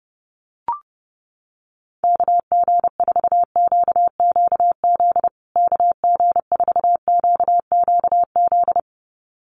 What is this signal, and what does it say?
Morse code, heard as a keyed tone, about 30 wpm, 700 Hz. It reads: KG4QQZ KG4QQZ
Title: USA Call Signs Repeated 2x 30wpm